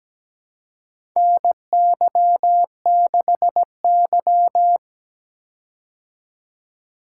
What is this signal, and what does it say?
Morse code, heard as a keyed tone, about 17 wpm, 700 Hz. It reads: NY6Y
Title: K1USN SST QSO Practice - 17wpm